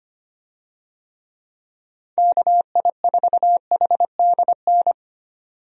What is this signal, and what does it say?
Morse code, heard as a keyed tone, about 25 wpm, 700 Hz. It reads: KI4HDN
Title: USA Call Signs - 6 Characters - 25wpm